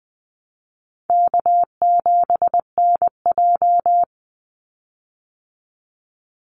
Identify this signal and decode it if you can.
Morse code, heard as a keyed tone, about 20 wpm, 700 Hz. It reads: K7NJ
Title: Easy Call Signs Repeated 2x - 20wpm